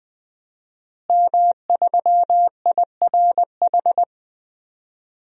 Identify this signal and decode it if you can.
Morse code, heard as a keyed tone, about 20 wpm, 700 Hz. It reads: M3IRH
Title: DXCC-100 Call Signs - 5 Characters - Repeated 2x - 20wpm